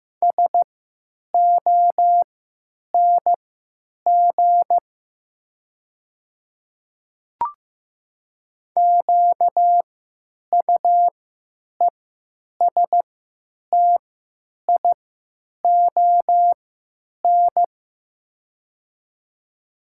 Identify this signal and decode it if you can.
Morse code, heard as a keyed tone, about 15 wpm, spaced as for 9 wpm, 700 Hz. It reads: SONG QUESTION
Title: 3x Character Spacing - Top 500 Words - 15wpm